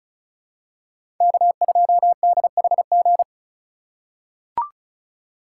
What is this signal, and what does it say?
Morse code, heard as a keyed tone, about 36 wpm, 700 Hz. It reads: K2DHG
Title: USA Call Signs - 5 Characters - 4x Character Spacing - 35wpm